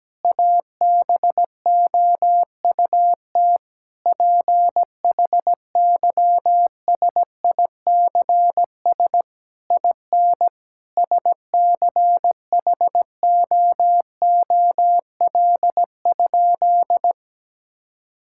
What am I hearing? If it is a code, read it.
Morse code, heard as a keyed tone, about 17 wpm, 700 Hz. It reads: ABOUT PHYSICS IN SCHOOL?
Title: Sentences from Top 1000 Words - Review Part 5 - 17wpm